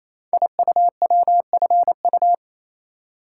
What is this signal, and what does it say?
Morse code, heard as a keyed tone, about 27 wpm, 700 Hz. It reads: IUWFU
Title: Code Groups - 5 Letters - 28wpm